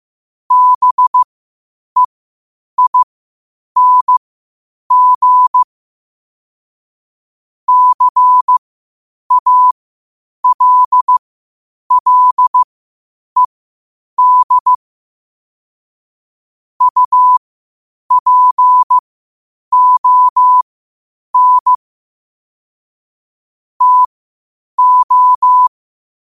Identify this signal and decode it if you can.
Morse code, heard as a keyed tone, about 15 wpm, spaced as for 8 wpm, 1000 Hz. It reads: BEING CALLED UPON TO
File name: qotdfiles / 8wpm